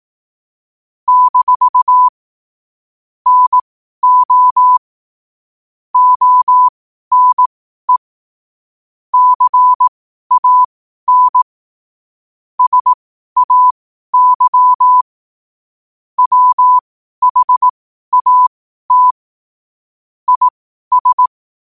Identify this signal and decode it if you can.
Morse code, heard as a keyed tone, about 18 wpm, spaced as for 12 wpm, 1000 Hz. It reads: - NO ONE CAN SAY WHAT IS
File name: qotdfiles / 12wpm